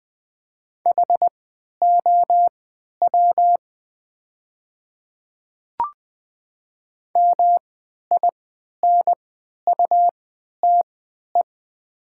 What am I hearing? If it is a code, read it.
Morse code, heard as a keyed tone, about 20 wpm, spaced as for 11 wpm, 700 Hz. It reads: HOW MINUTE
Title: 3x Character Spacing - Top 500 Words - 20wpm